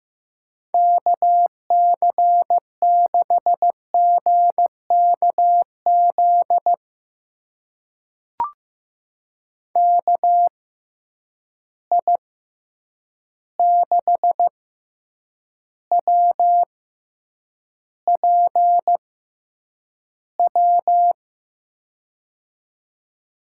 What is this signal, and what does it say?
Morse code, heard as a keyed tone, about 15 wpm, spaced as for 8 wpm, 700 Hz. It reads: KC6GKZ KI6WPW
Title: USA Call Signs - 6 Characters - 6x Character Spacing - 15wpm